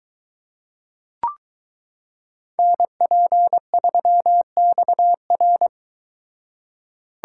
Morse code, heard as a keyed tone, about 23 wpm, 700 Hz. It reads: NP3XR